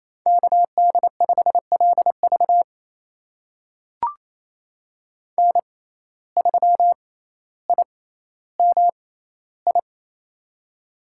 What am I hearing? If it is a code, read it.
Morse code, heard as a keyed tone, about 28 wpm, spaced as for 14 wpm, 700 Hz. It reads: KD5LV N3IMI
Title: USA Call Signs - 5 Characters - 6x Character Spacing - 28wpm